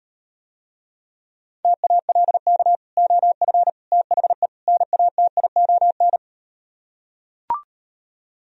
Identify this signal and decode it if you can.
Morse code, heard as a keyed tone, about 38 wpm, 700 Hz. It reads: TALK OF THE NATION